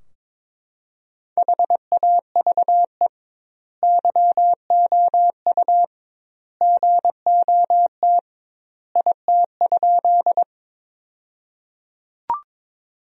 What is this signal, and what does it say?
Morse code, heard as a keyed tone, about 22 wpm, 700 Hz. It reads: HAVE YOU GOT IT?
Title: 2x Word-Spacing - Sentences from Top 300 Words - 22wpm